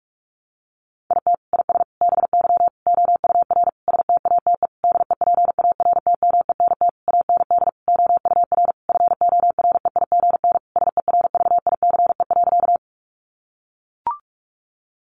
Morse code, heard as a keyed tone, about 44 wpm, 700 Hz. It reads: IT IS BY OUR STATE DEPARTMENT AND OUR FOREIGN SERVICE.
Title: Sentences from Top 1400 Words 45wpm